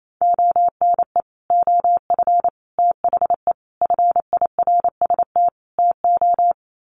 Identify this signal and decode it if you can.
Morse code, heard as a keyed tone, about 28 wpm, 700 Hz. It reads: ONE OF THE FIRST TO
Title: Sets of 5 Words Encore 28wpm